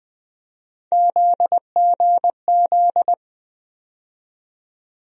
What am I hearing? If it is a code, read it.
Morse code, heard as a keyed tone, about 21 wpm, 700 Hz. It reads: ZGZ
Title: USA Call Sign Postfixes - 3 Letters 20wpm